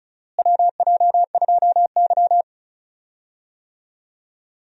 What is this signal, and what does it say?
Morse code, heard as a keyed tone, about 36 wpm, 700 Hz. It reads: WJ2Y
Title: USA Call Signs - 4 Characters - Repeated 2x - 35wpm